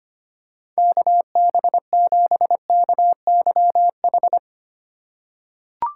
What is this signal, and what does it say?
Morse code, heard as a keyed tone, about 25 wpm, 700 Hz. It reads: KB7KYH
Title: USA Call Signs - 6 Characters - 3x Character Spacing - 25wpm